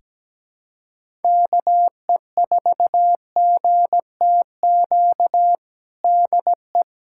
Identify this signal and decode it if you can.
Morse code, heard as a keyed tone, about 17 wpm, 700 Hz. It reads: KE4GTQ DE
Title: USA Callsign DE Callsign - 6 Characters - 17wpm